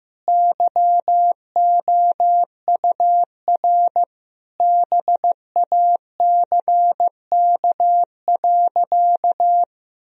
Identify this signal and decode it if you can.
Morse code, heard as a keyed tone, about 15 wpm, 700 Hz. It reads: YOUR BACK.
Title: Sentences from Top 1000 Words - Review Part 1 - 15wpm